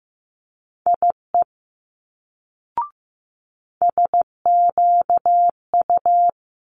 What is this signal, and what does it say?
Morse code, heard as a keyed tone, about 15 wpm, 700 Hz. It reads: IE SQU
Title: Multi-letter Phonemes Quick Fire 15wpm